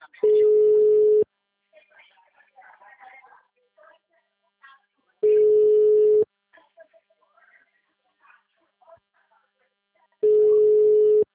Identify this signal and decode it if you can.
Ringback tone (Europe, 425 Hz), with speech.